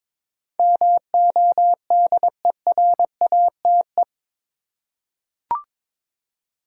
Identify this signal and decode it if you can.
Morse code, heard as a keyed tone, about 22 wpm, 700 Hz. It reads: MODERATE
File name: practice / 22wpm / Eight Letter Words - 22wpm